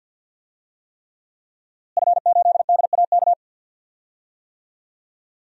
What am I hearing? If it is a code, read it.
Morse code, heard as a keyed tone, about 51 wpm, 700 Hz. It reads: F8DAK
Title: DXCC-100 Call Signs - 5 Characters - Repeated 3x - 50wpm